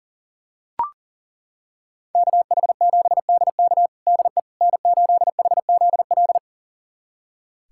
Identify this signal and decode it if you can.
Morse code, heard as a keyed tone, about 40 wpm, 700 Hz. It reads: KH7DK DE N8HZL
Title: USA Callsign DE Callsign - 5 Characters - 40wpm